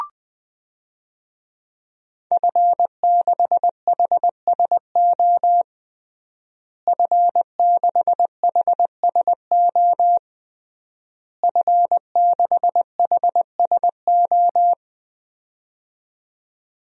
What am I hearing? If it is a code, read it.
Morse code, heard as a keyed tone, about 21 wpm, 700 Hz. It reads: F6HSO F6HSO F6HSO
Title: DXCC-100 Call Signs - 5 Characters - Repeated 3x - 20wpm